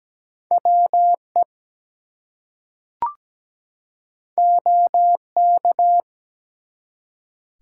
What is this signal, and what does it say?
Morse code, heard as a keyed tone, about 17 wpm, 700 Hz. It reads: WE OK
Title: Two Letter Words - 17wpm